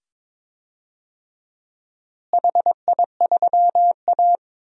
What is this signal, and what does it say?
Morse code, heard as a keyed tone, about 21 wpm, 700 Hz. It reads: HI3A
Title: DXCC-100 Call Signs - 4 Characters - Repeated 2x - 22wpm